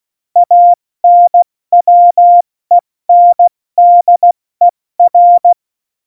Morse code, heard as a keyed tone, about 16 wpm, 700 Hz. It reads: ANWENDER
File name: Morsekurs / Witze / Witz 22. Januar 2019, 70 F80